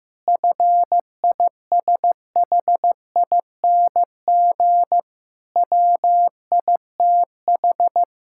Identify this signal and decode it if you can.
Morse code, heard as a keyed tone, about 15 wpm, 700 Hz. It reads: FISHING WITH
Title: Sentences from Top 1000 Words - Review Part 5 - 15wpm